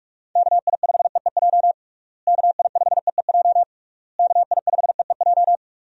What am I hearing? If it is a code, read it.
Morse code, heard as a keyed tone, about 46 wpm, 700 Hz. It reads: KI5EEJ KI5EEJ KI5EEJ
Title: USA Call Signs - 6 Characters - Repeated 3x - 45wpm